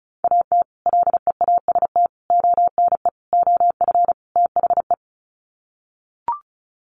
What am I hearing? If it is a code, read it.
Morse code, heard as a keyed tone, about 34 wpm, 700 Hz. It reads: AT LEAST ONE OF THE
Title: Sets of 5 Words Encore 35wpm